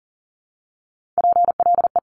Morse code, heard as a keyed tone, about 38 wpm, 690 Hz. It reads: PLE
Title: Top Three Letter Combinations in Words 40wpm